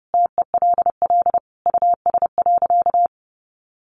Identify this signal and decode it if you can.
Morse code, heard as a keyed tone, about 30 wpm, 700 Hz. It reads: TELL US.
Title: Sentences from Top 1400 Words 30wpm